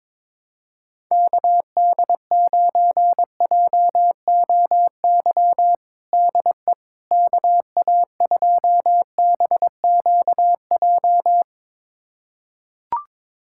Morse code, heard as a keyed tone, about 22 wpm, 700 Hz. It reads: KD9JOY DE KA2BQJ
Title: USA Callsign DE Callsign - 6 Characters - 22wpm